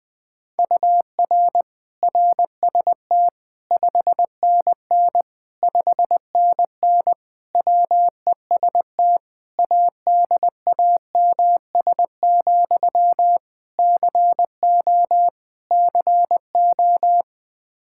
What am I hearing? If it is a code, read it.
Morse code, heard as a keyed tone, about 21 wpm, 700 Hz. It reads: UR RST 5NN 5NN WEST ADAMS, CO CO
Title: Practice QSO - Exchange 1 - Beginning - 20wpm